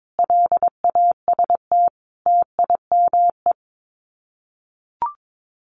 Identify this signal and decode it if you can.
Morse code, heard as a keyed tone, about 22 wpm, 700 Hz. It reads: LAST TIME
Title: Sets of 5 Words Encore 22wpm